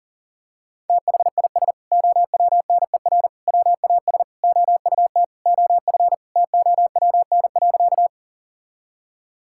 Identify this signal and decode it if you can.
Morse code, heard as a keyed tone, about 37 wpm, 700 Hz. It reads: THIS OWNER WAS OUT OF TOWN.
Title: Sentences from Top 1600 Words 40wpm